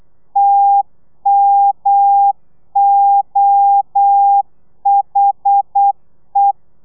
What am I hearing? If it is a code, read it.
Morse code, heard as a keyed tone, about 8 wpm, 800 Hz. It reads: TMOHE
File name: telegrafia / i2tfj / Lez1 35